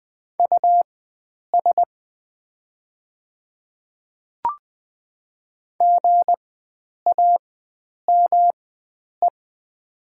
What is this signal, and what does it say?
Morse code, heard as a keyed tone, about 20 wpm, spaced as for 9 wpm, 700 Hz. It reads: US GAME
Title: 4x Character Spacing - Top 500 Words - 20wpm